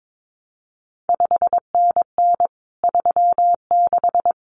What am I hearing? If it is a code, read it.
Morse code, heard as a keyed tone, about 22 wpm, 700 Hz. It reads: 5NN 36
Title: CQ WW DX Contest Exchange Encore - 22wpm